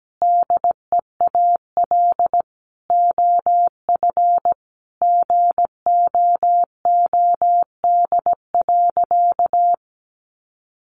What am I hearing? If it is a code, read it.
Morse code, heard as a keyed tone, about 17 wpm, 700 Hz. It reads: DEAL OF GOOD.
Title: Sentences from Top 900 Words 17wpm